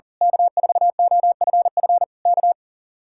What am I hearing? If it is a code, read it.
Morse code, heard as a keyed tone, about 37 wpm, 700 Hz. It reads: K4OFF K
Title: Speed-Racing Common QSO Phrases Encore - 40wpm